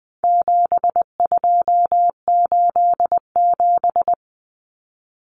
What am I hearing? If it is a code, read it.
Morse code, heard as a keyed tone, about 20 wpm, 700 Hz. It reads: 7287
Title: Common Cut Numbers - 4-Digit 20wpm